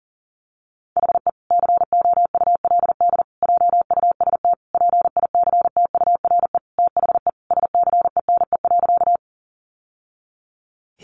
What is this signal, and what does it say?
Morse code, heard as a keyed tone, about 37 wpm, 700 Hz. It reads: HE COULD JUST PICTURE THE SCENE.
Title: Sentences from Top 1100 Words 40wpm